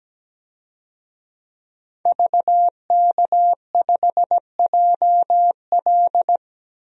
Morse code, heard as a keyed tone, about 17 wpm, 700 Hz. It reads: VK5JL
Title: DXCC-100 Call Signs - 5 Characters - Repeated 3x - 17wpm